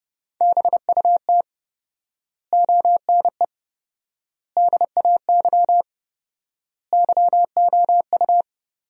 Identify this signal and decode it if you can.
Morse code, heard as a keyed tone, about 30 wpm, 700 Hz. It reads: BUT ONE DAY YOU
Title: 4x Word-Spacing - Sentences from Top 400 Words - 30wpm